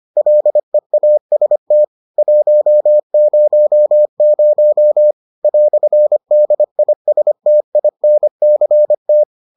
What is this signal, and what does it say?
Morse code, heard as a keyed tone, about 25 wpm, 600 Hz. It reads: LEAST 100 "DISTINCT